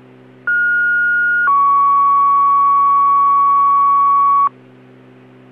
Two-tone page, heard as a one-second tone, then a three-second tone, about 1430 then 1120 Hz. It is heard from 0.5 to 4.5 s.